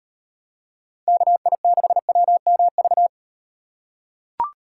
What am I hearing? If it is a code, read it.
Morse code, heard as a keyed tone, about 36 wpm, 700 Hz. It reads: KI6WMV